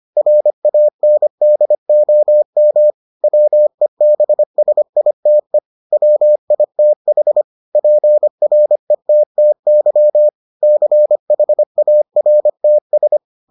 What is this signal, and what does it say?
Morse code, heard as a keyed tone, about 25 wpm, 600 Hz. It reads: RANDOM WEBSITE WITH PRETTY CHARTS